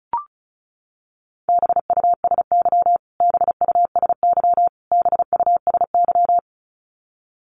Morse code, heard as a keyed tone, about 34 wpm, 700 Hz. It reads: BUSY BUSY BUSY
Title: Top 500 Words Repeated 35wpm